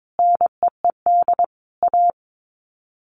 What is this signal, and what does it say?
Morse code, heard as a keyed tone, about 22 wpm, 700 Hz. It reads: NEED A
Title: Sets of 5 Words Encore 22wpm